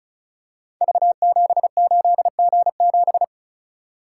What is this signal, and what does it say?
Morse code, heard as a keyed tone, about 36 wpm, 700 Hz. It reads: V78G7
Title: Code Groups - 5 Letter:Numbers - 35wpm